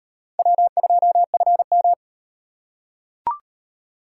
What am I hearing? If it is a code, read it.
Morse code, heard as a keyed tone, about 38 wpm, 700 Hz. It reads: W2FM